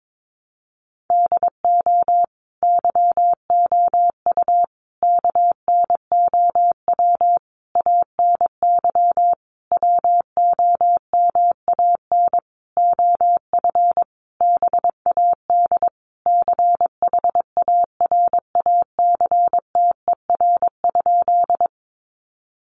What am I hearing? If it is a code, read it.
Morse code, heard as a keyed tone, about 22 wpm, 700 Hz. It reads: DO YOU KNOW ANY WOMAN OF BAD CHARACTER?
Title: Sentences from Top 1200 Words 22wpm